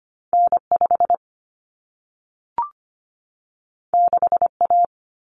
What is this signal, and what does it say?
Morse code, heard as a keyed tone, about 25 wpm, 700 Hz. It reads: N5 6A
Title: Common Cut Numbers - 2-Digit 25wpm